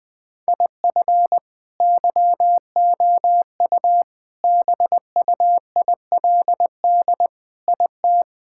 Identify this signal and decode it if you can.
Morse code, heard as a keyed tone, about 21 wpm, 700 Hz. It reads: IF YOU BUILD IT
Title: Sentences with 7-Word Phrases - 20wpm